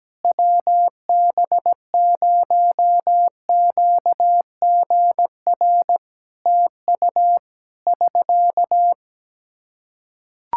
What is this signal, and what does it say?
Morse code, heard as a keyed tone, about 17 wpm, 700 Hz. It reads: WB0QGR TU <SK>
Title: Practice QSO - Exchange 4 - Ending - 17wpm